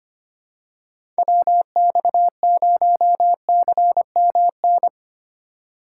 Morse code, heard as a keyed tone, about 25 wpm, 700 Hz. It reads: WX0CMN
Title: USA Call Signs - 6 Characters - 3x Character Spacing - 25wpm